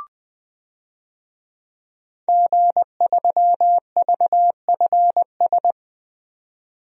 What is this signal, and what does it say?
Morse code, heard as a keyed tone, about 20 wpm, 700 Hz. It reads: G3VFS
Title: DXCC-100 Call Signs - 5 Characters - 20wpm